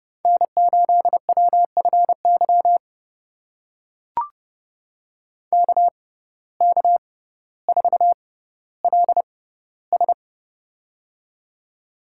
Morse code, heard as a keyed tone, about 30 wpm, spaced as for 15 wpm, 700 Hz. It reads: N8WFY KK4LS
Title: USA Call Signs - 5 Characters - 6x Character Spacing - 30wpm